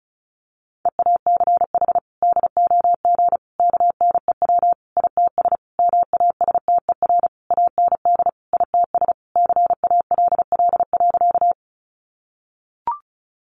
Morse code, heard as a keyed tone, about 34 wpm, 700 Hz. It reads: EACH DOG KNEW ITS MASTER AND ITS CALL.
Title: Sentences from Top 1300 Words 35wpm